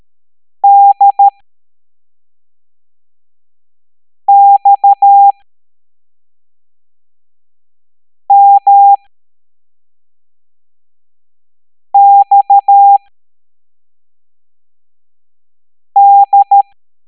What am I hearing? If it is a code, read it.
Morse code, heard as a keyed tone, about 13 wpm, spaced as for 3 wpm, 800 Hz. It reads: DXMXD
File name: hamdata / L02-EX01-MDX